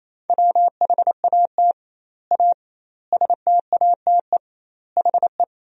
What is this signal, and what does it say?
Morse code, heard as a keyed tone, about 28 wpm, 700 Hz. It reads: WHAT A STATE HE
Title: 2x Word-Spacing - Sentences from Top 400 Words - 28wpm